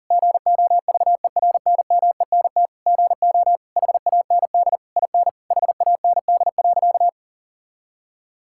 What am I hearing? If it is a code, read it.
Morse code, heard as a keyed tone, about 40 wpm, 690 Hz. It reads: GOVERNMENT GO HAND IN HAND.
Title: Sentences from Top 1000 Words - Review Part 5 - 40wpm